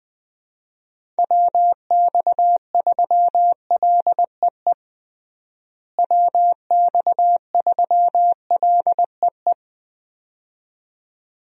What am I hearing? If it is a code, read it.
Morse code, heard as a keyed tone, about 20 wpm, 700 Hz. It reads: WX3LEE WX3LEE
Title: USA Call Signs - 6 Characters - Repeated 2x - 20wpm